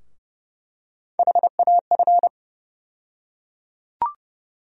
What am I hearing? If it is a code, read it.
Morse code, heard as a keyed tone, about 30 wpm, 700 Hz. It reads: HAF
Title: USA Call Sign Postfixes - 3 Letters 30wpm